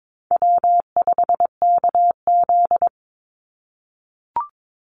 Morse code, heard as a keyed tone, about 22 wpm, 700 Hz. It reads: W5KZ
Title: USA Call Signs 22wpm